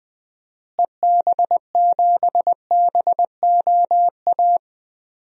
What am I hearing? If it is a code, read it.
Morse code, heard as a keyed tone, about 20 wpm, 700 Hz. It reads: EB7BOA
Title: DXCC-100 Call Signs - 6 Characters - Repeated 3x - 20wpm